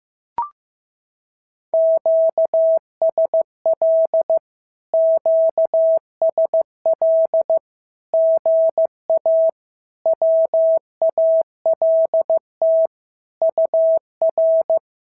Morse code, heard as a keyed tone, about 15 wpm, 650 Hz. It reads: QSL QSL GA WALT UR
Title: SOTA QSO Practice 15wpm